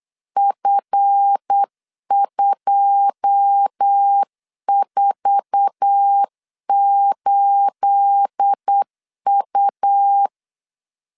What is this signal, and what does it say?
Morse code, heard as a keyed tone, about 8 wpm, 800 Hz. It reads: F248U
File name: LEZIONI / 9 28 8wpm 3512